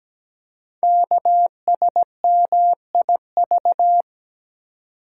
Morse code, heard as a keyed tone, about 17 wpm, 700 Hz. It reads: KSMIV
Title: Code Groups - 5 Letters - 17wpm